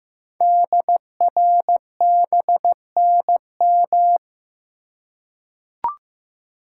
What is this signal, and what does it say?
Morse code, heard as a keyed tone, about 15 wpm, 700 Hz. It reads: DRBNM
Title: Code Groups - 5 Letters - 15wpm